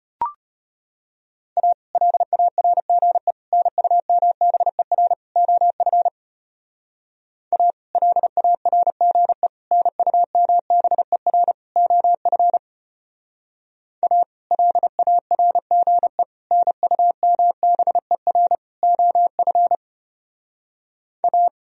Morse code, heard as a keyed tone, about 31 wpm, 700 Hz. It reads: A LARGE NUMBER OF A LARGE NUMBER OF A LARGE NUMBER OF A
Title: Speed-Racing - Sets of 4 Words 25wpm